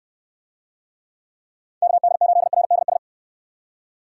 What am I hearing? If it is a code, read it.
Morse code, heard as a keyed tone, about 64 wpm, 700 Hz. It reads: KD8RDS